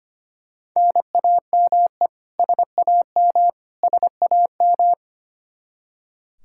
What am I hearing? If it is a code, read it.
Morse code, heard as a keyed tone, about 25 wpm, 700 Hz. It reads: NAME SAM SAM
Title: Practice QSO - Exchange 1 - Beginning - 25wpm